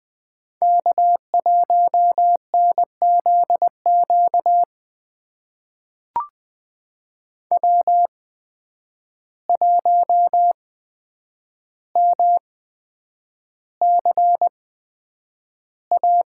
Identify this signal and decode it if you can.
Morse code, heard as a keyed tone, about 20 wpm, spaced as for 9 wpm, 700 Hz. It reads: K1NZQ W1MCA
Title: USA Call Signs - 5 Characters - 8x Character Spacing - 20wpm